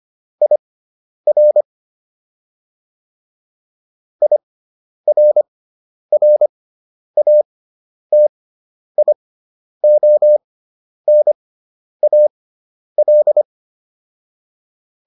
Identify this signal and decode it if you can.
Morse code, heard as a keyed tone, about 25 wpm, spaced as for 10 wpm, 600 Hz. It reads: IR IRRATIONAL